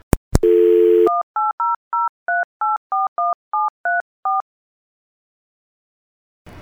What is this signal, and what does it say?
DTMF, heard as telephone touch tones, tones of about 180 ms. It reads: 18003841734